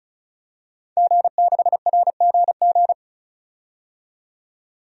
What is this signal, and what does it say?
Morse code, heard as a keyed tone, about 36 wpm, 700 Hz. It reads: G6RGG
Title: DXCC-100 Call Signs - 5 Characters - 35wpm